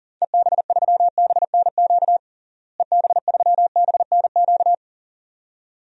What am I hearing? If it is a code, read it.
Morse code, heard as a keyed tone, about 40 wpm, 700 Hz. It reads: EB3BNQ EB3BNQ
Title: DXCC-100 Call Signs - 6 Characters - Repeated 2x - 40wpm